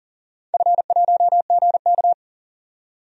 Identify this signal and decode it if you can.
Morse code, heard as a keyed tone, about 40 wpm, 700 Hz. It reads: F1GK
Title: DXCC-100 Call Signs - 4 Characters - 40wpm